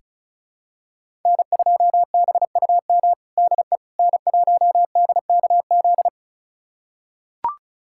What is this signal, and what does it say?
Morse code, heard as a keyed tone, about 34 wpm, 700 Hz. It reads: N2BUM DE N1DKZ
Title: USA Callsign DE Callsign - 5 Characters - 35wpm